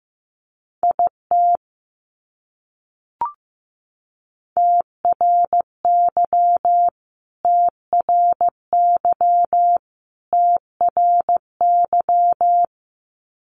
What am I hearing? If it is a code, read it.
Morse code, heard as a keyed tone, about 15 wpm, 700 Hz. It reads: IT TRY TRY TRY
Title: Top 200 Words Repeated 15wpm